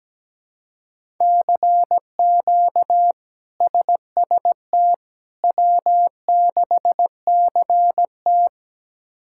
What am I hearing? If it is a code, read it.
Morse code, heard as a keyed tone, about 17 wpm, 700 Hz. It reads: CQ SST W6CT
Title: K1USN SST QSO Practice - 17wpm